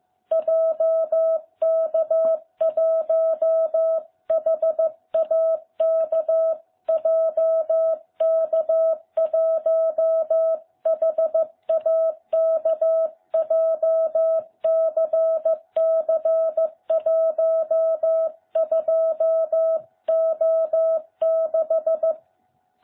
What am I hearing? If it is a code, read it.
Morse code, heard as a keyed tone, about 15 wpm, 660 Hz. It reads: JK1HAK JK1HAK JCC12O6